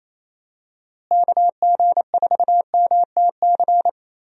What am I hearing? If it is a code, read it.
Morse code, heard as a keyed tone, about 28 wpm, 700 Hz. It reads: KG4MTC
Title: USA Call Signs - 6 Characters - Repeated 3x - 28wpm